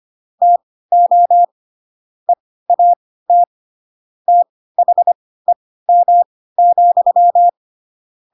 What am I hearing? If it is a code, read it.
Morse code, heard as a keyed tone, about 25 wpm, spaced as for 16 wpm, 700 Hz. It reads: TO EAT THEM,
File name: Pinocchio / Pinocchio ch7 10 25 ik1hge